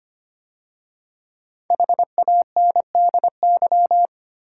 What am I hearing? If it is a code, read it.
Morse code, heard as a keyed tone, about 25 wpm, 700 Hz. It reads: HANDY
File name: 5 Letter Words - First 3 Letters Given - 25wpm